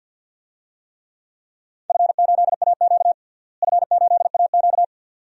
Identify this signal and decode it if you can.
Morse code, heard as a keyed tone, about 47 wpm, 700 Hz. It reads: F8AQ F8AQ
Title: DXCC-100 Call Signs - 4 Characters - Repeated 2x - 50wpm